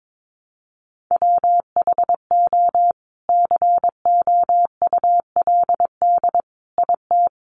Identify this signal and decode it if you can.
Morse code, heard as a keyed tone, about 22 wpm, 700 Hz. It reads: WHO COULD IT
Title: Sentences from Top 1300 Words 22wpm